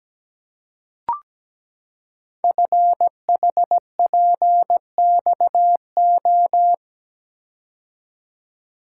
Morse code, heard as a keyed tone, about 17 wpm, 700 Hz. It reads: FHPXO
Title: Code Groups - 5 Letters - 17wpm